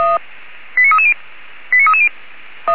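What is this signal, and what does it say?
Five-tone selective call (ZVEI), heard as quick runs of five tones, tones of about 70 ms. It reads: 89330 89330
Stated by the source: ZVEI five-tone selective call received by SDR.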